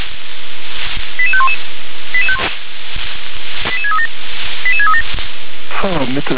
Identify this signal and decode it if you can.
Five-tone selective call (ZVEI), heard as quick runs of five tones, tones of about 70 ms, with speech. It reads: 88511 88637 88637